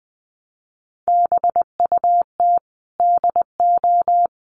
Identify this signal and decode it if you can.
Morse code, heard as a keyed tone, about 20 wpm, 700 Hz. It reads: BUT DO
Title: Sentences from Top 600 Words 20wpm